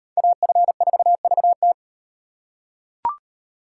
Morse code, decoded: AF4VT

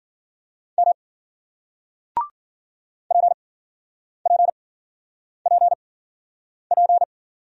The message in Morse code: M PPPP

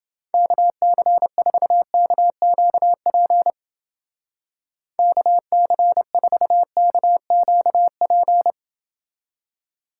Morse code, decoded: KC4KQP KC4KQP

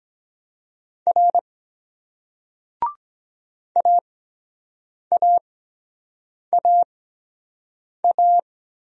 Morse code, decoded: R AAAA

